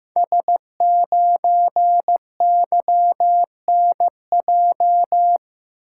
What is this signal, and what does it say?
Morse code, heard as a keyed tone, about 15 wpm, 700 Hz. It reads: S9YNJ
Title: Code Groups - 5 Letter:Numbers - 15wpm